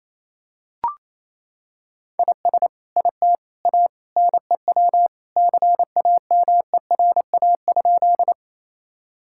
Morse code, heard as a keyed tone, about 28 wpm, 700 Hz. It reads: IS IT A NEW CAMERA?
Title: Sentences from Top 1000 Words - Review Part 5 - 28wpm